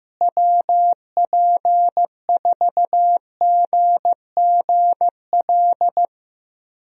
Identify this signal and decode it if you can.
Morse code, heard as a keyed tone, about 15 wpm, 700 Hz. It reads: WP4GGL
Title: USA Call Signs - 6 Characters - Repeated 3x - 15wpm